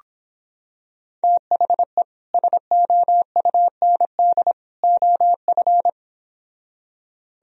Morse code, heard as a keyed tone, about 26 wpm, 700 Hz. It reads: THE SOUND OF